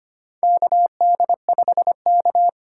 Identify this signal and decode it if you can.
Morse code, heard as a keyed tone, about 25 wpm, 700 Hz. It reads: KD5K
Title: USA Call Signs - 4 Characters - 4x Character Spacing - 25wpm